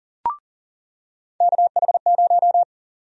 Morse code, heard as a keyed tone, about 40 wpm, 700 Hz. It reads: KH0
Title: USA Call Sign Prefixes - 40wpm